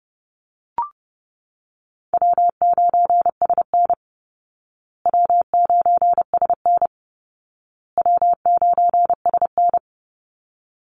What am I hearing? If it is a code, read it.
Morse code, heard as a keyed tone, about 30 wpm, 700 Hz. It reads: W9SN W9SN W9SN
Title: Easy Call Signs Repeated 3x - 30wpm